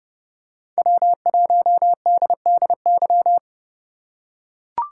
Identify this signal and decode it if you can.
Morse code, heard as a keyed tone, about 30 wpm, 700 Hz. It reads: W1DDY